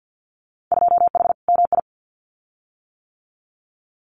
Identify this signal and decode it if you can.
Morse code, heard as a keyed tone, about 50 wpm, 700 Hz. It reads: 2H MI
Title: Winter Field Day Exchanges Encore 50wpm